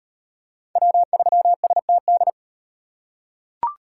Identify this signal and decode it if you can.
Morse code, heard as a keyed tone, about 38 wpm, 700 Hz. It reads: W3STD